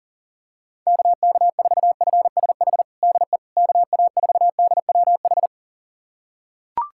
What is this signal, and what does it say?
Morse code, heard as a keyed tone, about 37 wpm, 690 Hz. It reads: KK4FSH DE KA4DWH
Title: USA Callsign DE Callsign - 6 Characters - 40wpm